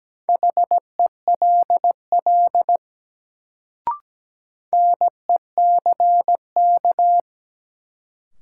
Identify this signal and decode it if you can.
Morse code, heard as a keyed tone, about 17 wpm, 700 Hz. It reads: HELL NECK
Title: Sets of 5 Words Encore - New Word Drill - 17wpm